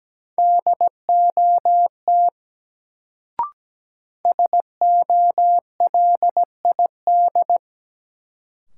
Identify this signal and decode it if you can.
Morse code, heard as a keyed tone, about 17 wpm, 700 Hz. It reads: DOT SOLID
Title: Sentences from Top 1500 Words - New Word Drill - 17wpm